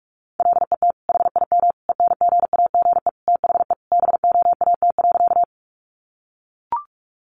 Morse code, heard as a keyed tone, about 44 wpm, 700 Hz. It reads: LET HIM ENGAGE THE BOAT.
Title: Sentences from Top 1500 Words 45wpm